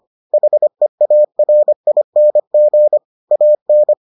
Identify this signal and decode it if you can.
Morse code, heard as a keyed tone, about 25 wpm, 600 Hz. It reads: HEARING AN